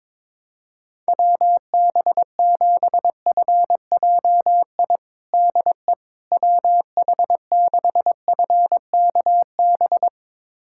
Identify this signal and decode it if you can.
Morse code, heard as a keyed tone, about 22 wpm, 700 Hz. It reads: WB7FJI DE WH6FKB
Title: USA Callsign DE Callsign - 6 Characters - 22wpm